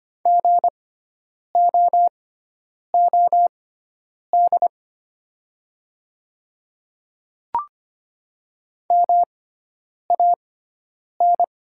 Morse code, heard as a keyed tone, about 25 wpm, spaced as for 9 wpm, 700 Hz. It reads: GOOD MAN
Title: 6x Character Spacing - Top 300 Words - 25wpm